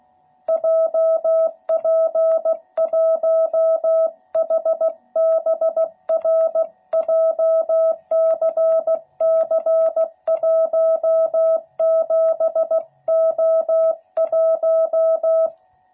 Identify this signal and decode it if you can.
Morse code, heard as a keyed tone, about 15 wpm, 660 Hz. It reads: JP1HBR JCC17O1
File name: jh8cbh / 2021cw / cw12b